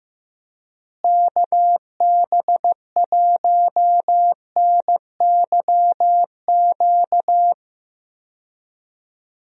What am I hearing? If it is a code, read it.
Morse code, heard as a keyed tone, about 15 wpm, 700 Hz. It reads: KB1NYQ
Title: USA Call Signs - 6 Characters - Repeated 2x - 15wpm